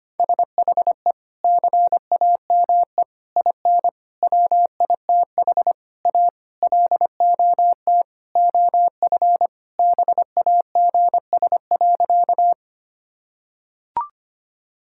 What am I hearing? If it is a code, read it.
Morse code, heard as a keyed tone, about 25 wpm, 700 Hz. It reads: SHE CAME IN WITH A LOT OF BAGS.
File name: practice / 25wpm / Sentences with 4-Word Phrases - 25wpm